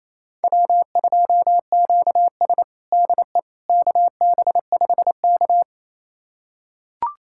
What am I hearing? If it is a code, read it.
Morse code, heard as a keyed tone, about 28 wpm, 700 Hz. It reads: W2QS DE KB5K
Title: USA Callsign DE Callsign - 4 Characters - 28wpm